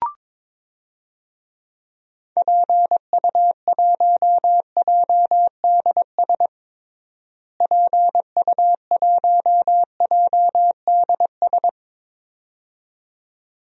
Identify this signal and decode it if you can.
Morse code, heard as a keyed tone, about 22 wpm, 700 Hz. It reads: PU1JDS PU1JDS
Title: DXCC-100 Call Signs - 6 Characters - Repeated 2x - 22wpm